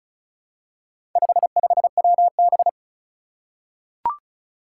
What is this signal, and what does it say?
Morse code, heard as a keyed tone, about 36 wpm, 700 Hz. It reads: 55WB
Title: Four Challenging Letter-Number Combinations 35wpm